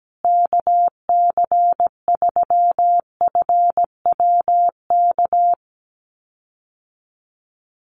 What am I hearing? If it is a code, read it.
Morse code, heard as a keyed tone, about 17 wpm, 700 Hz. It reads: KC3FWK
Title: USA Call Signs Repeated 2x 17wpm